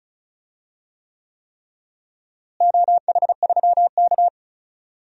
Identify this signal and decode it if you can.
Morse code, heard as a keyed tone, about 34 wpm, 700 Hz. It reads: OH3K
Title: DXCC-100 Call Signs - 4 Characters - Repeated 3x - 35wpm